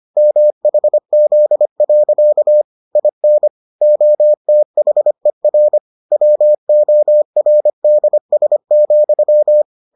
Morse code, heard as a keyed tone, about 25 wpm, 600 Hz. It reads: MHZ. IN OTHER WORDS,